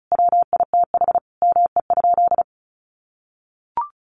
Morse code, heard as a keyed tone, about 34 wpm, 700 Hz. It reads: WITH ME?
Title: Sentences from Top 900 Words 35wpm